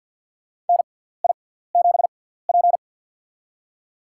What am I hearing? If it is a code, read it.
Morse code, heard as a keyed tone, about 47 wpm, spaced as for 17 wpm, 700 Hz. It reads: NI7P
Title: USA Call Signs - 4 Characters - 6x Character Spacing - 50wpm